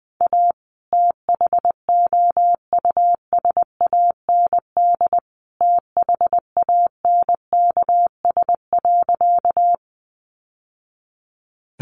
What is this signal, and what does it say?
Morse code, heard as a keyed tone, about 20 wpm, 700 Hz. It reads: A THOUSAND THANKS.
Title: Sentences from Top 600 Words 20wpm